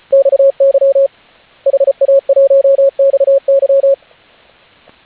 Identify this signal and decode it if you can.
Morse code, heard as a keyed tone, about 34 wpm, 550 Hz, with radio noise. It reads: XY HA1XY